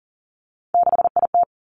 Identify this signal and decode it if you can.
Morse code, heard as a keyed tone, about 41 wpm, 700 Hz. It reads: 6IT
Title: Three Challenging Letter-Number Combinations 40wpm